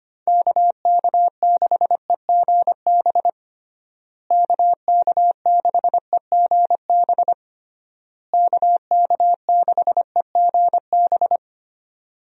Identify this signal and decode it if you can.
Morse code, heard as a keyed tone, about 25 wpm, 700 Hz. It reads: KK6EGB KK6EGB KK6EGB
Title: USA Call Signs - 6 Characters - Repeated 3x - 25wpm